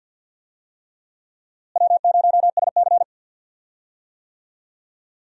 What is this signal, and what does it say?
Morse code, heard as a keyed tone, about 46 wpm, 700 Hz. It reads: W0SC
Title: USA Call Signs - 4 Characters - Repeated 3x - 50wpm